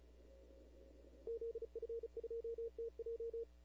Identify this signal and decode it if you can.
Morse code, heard as a keyed tone, about 40 wpm, 470 Hz. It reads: ZF2TJ
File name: CW / 2005-k5tr arrl cw dx -19:00